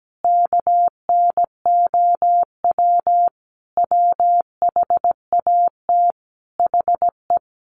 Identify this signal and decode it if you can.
Morse code, heard as a keyed tone, about 17 wpm, 700 Hz. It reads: KNOW WHAT HE
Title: Sets of 5 Words Encore2 17wpm